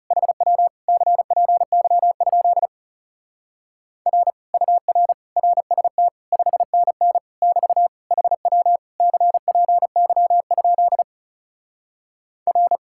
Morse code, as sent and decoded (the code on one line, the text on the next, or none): .... .-- / -.-. .--. -.-- ..--.. / .-. / ..- .-. / .-. ... - / ..... -. -. / -...- / .... .-- / -.-. .--. -.-- ..--.. / .-.
HW CPY? R UR RST 5NN <BT> HW CPY? R